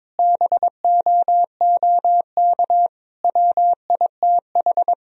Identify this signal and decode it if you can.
Morse code, heard as a keyed tone, about 22 wpm, 700 Hz. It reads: BOOK WITH
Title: Sentences from Top 1000 Words - Review Part 1 - 22wpm